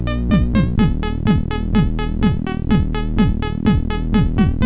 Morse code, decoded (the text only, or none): none